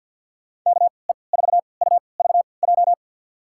Morse code, heard as a keyed tone, about 53 wpm, spaced as for 29 wpm, 700 Hz. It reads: KE4UVJ
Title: USA Call Signs - 6 Characters - 3x Character Spacing - 50wpm